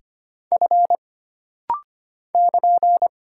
Morse code, heard as a keyed tone, about 25 wpm, 700 Hz. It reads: F <KN>
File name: Single Character (Letter:Number:Punctuation:Prosign) - Rapid-Fire - 25wpm